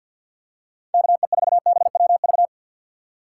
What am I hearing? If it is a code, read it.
Morse code, heard as a keyed tone, about 50 wpm, 700 Hz. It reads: KE4BWV
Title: USA Call Signs - 6 Characters - 3x Character Spacing - 50wpm